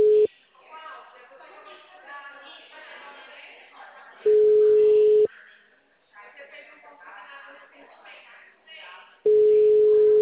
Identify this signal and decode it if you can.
Ringback tone (Europe, 425 Hz), with speech.